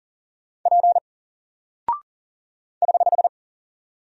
Morse code, heard as a keyed tone, about 39 wpm, 700 Hz. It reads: P <HH>
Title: Single Character (Letter:Number:Punctuation:Prosign) 40wpm